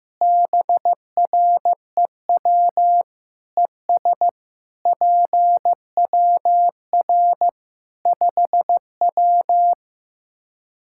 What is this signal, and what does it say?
Morse code, heard as a keyed tone, about 15 wpm, 700 Hz. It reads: BREW ES PWR 5W
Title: K1USN SST QSO Practice 15wpm